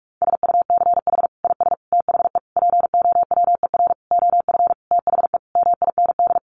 Morse code, heard as a keyed tone, about 44 wpm, 700 Hz. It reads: SUCH IS THE POWER OF THE MIND
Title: Sentences from Top 1400 Words 45wpm